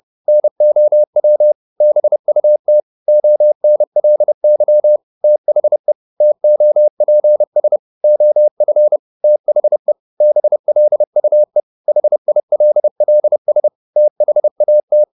Morse code, heard as a keyed tone, about 30 wpm, 600 Hz. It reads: NOW BUT ONLY THE TOPS OF THE BLUE HILLS THAT
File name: the old man and the sea / the old man and the sea 1 30wpm